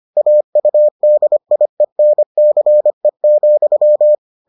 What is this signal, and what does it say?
Morse code, heard as a keyed tone, about 25 wpm, 600 Hz. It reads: AUDIENCE,